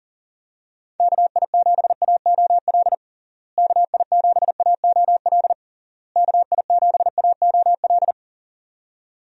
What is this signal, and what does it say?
Morse code, heard as a keyed tone, about 40 wpm, 700 Hz. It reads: KI7AOL KI7AOL KI7AOL
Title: USA Call Signs - 6 Characters - Repeated 3x - 40wpm